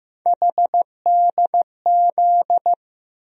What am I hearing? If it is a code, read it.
Morse code, heard as a keyed tone, about 16 wpm, 700 Hz. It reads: HDZ
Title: USA Call Sign Postfixes - 3 Letters 15wpm